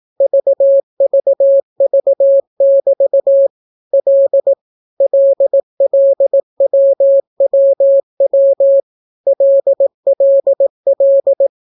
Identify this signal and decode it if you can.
Morse code, heard as a keyed tone, about 18 wpm, 550 Hz. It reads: VVV<BT> L LLWWW LLL